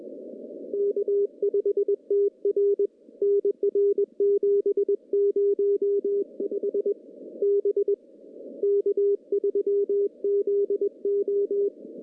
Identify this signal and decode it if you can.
Morse code, heard as a keyed tone, about 21 wpm, 400 Hz. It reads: K5TR NR 705 B K3ZO
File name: contests / CW-SS-k5tr- -1400